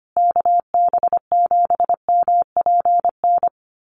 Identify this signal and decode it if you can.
Morse code, heard as a keyed tone, about 25 wpm, 700 Hz. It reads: KB7MPN
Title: USA Call Signs Repeated 3x 25wpm